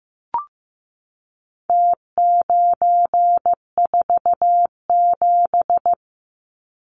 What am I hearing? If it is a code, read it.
Morse code, heard as a keyed tone, about 15 wpm, 700 Hz. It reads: T947
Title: Numbers - Contest Style - 4-Digit 15wpm